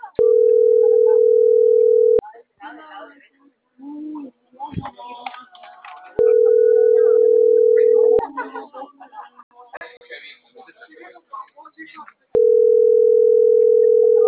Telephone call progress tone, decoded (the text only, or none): ringback tone